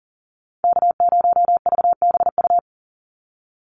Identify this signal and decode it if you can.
Morse code, heard as a keyed tone, about 40 wpm, 700 Hz. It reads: K0VBU
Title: Call Signs - 40wpm